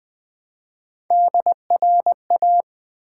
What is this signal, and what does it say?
Morse code, heard as a keyed tone, about 20 wpm, 700 Hz. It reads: DRA
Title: Build A Word - 1 Letter At A Time - 4-Letter Words - 20wpm